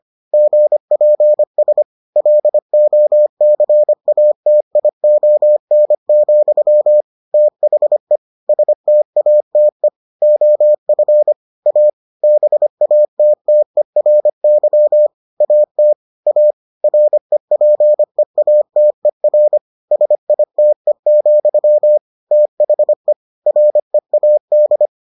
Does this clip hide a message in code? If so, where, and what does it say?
Yes: Morse code from 0.3 to 24.9 s, heard as a keyed tone, about 25 wpm, 600 Hz. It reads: GPS LOCATION, THE STATE OF A BATTERY AT A REPEATER SITE, THE READ